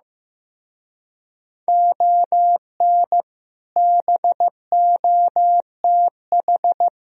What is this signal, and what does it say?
Morse code, heard as a keyed tone, about 16 wpm, 700 Hz. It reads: ON BOTH
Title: Speed-Racing - Sets of 4 Words 15wpm